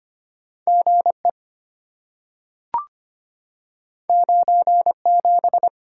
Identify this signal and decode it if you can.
Morse code, heard as a keyed tone, about 25 wpm, 700 Hz. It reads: GE 97
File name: Two Letter-Number Combinations - 25wpm